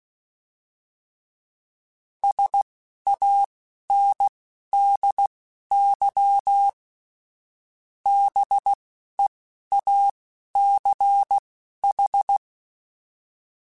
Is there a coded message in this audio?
Yes: Morse code from 2.2 to 12.4 s, heard as a keyed tone, about 16 wpm, spaced as for 11 wpm, 800 Hz. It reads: SANDY BEACH